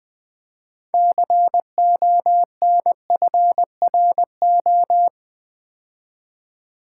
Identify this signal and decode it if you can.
Morse code, heard as a keyed tone, about 20 wpm, 700 Hz. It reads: CONFRO
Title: Build A Word - 3 Letters At A Time - 8-Letter Words - 20wpm